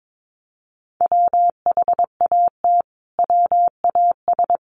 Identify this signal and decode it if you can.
Morse code, heard as a keyed tone, about 22 wpm, 700 Hz. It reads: WHAT WAS
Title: Sentences from Top 1400 Words 22wpm